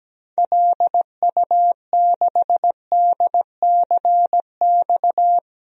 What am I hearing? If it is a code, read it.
Morse code, heard as a keyed tone, about 17 wpm, 700 Hz. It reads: LU6DCX
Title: DXCC-100 Call Signs - 6 Characters - 17wpm